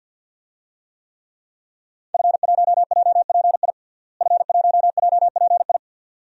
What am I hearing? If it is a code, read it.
Morse code, heard as a keyed tone, about 47 wpm, 700 Hz. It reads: F1JPI F1JPI
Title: DXCC-100 Call Signs - 5 Characters - Repeated 2x - 50wpm